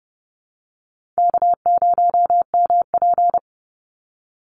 Morse code, heard as a keyed tone, about 30 wpm, 700 Hz. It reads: K0MP
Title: Easy Call Signs - 30wpm